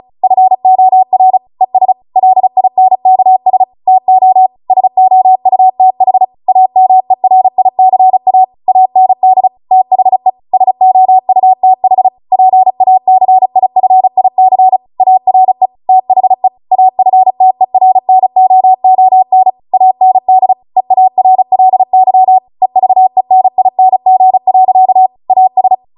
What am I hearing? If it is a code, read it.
Morse code, heard as a keyed tone, about 36 wpm, 750 Hz. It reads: FOR ES LINKS TO SOUTH AMERICA AND THE SOUTH PACIFIC ARE THE AFTERNOON AND EARLY EVENING. AS